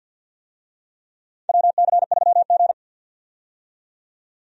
Morse code, heard as a keyed tone, about 51 wpm, 700 Hz. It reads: WC3G
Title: USA Call Signs - 4 Characters - 50wpm